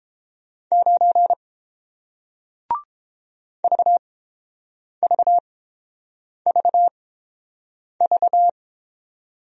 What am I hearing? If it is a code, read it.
Morse code, heard as a keyed tone, about 30 wpm, spaced as for 7 wpm, 700 Hz. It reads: 9 VVVV